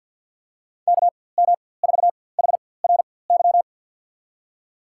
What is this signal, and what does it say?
Morse code, heard as a keyed tone, about 48 wpm, spaced as for 23 wpm, 700 Hz. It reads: KM4HRY
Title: USA Call Signs - 4x Character Spacing - 50wpm